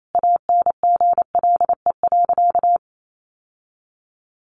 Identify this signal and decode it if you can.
Morse code, heard as a keyed tone, about 28 wpm, 700 Hz. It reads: ANGLE.
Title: Sentences from Top 1500 Words 28wpm